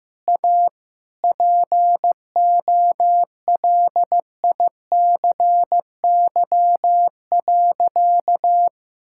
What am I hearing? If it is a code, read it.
Morse code, heard as a keyed tone, about 15 wpm, 700 Hz. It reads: A POLICY.
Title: Sentences from Top 1000 Words - Review Part 6 - 15wpm